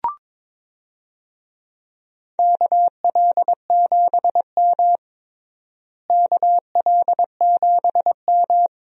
Morse code, heard as a keyed tone, about 21 wpm, 700 Hz. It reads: KL7M KL7M
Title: DXCC-100 Call Signs - 4 Characters - Repeated 3x - 22wpm